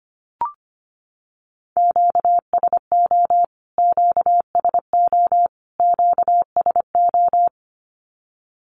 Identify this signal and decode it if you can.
Morse code, heard as a keyed tone, about 25 wpm, 700 Hz. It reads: QSO QSO QSO
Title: Q-Codes Repeated 3x 25wpm